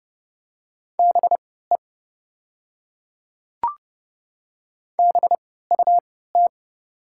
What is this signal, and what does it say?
Morse code, heard as a keyed tone, about 30 wpm, spaced as for 17 wpm, 700 Hz. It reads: BE BUT
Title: 3x Character Spacing - Top 100 Words - 30wpm